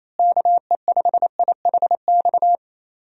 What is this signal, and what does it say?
Morse code, heard as a keyed tone, about 28 wpm, 700 Hz. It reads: KE5IHX
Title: USA Call Signs - 6 Characters - Repeated 3x - 28wpm